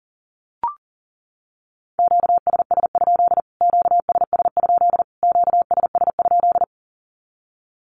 Morse code, heard as a keyed tone, about 40 wpm, 700 Hz. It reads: QSS? QSS? QSS?
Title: Q-Codes Repeated 3x 40wpm